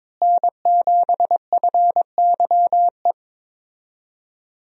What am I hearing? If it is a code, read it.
Morse code, heard as a keyed tone, about 22 wpm, 700 Hz. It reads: N7FYE
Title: USA Call Signs - 5 Characters - Repeated 3x - 22wpm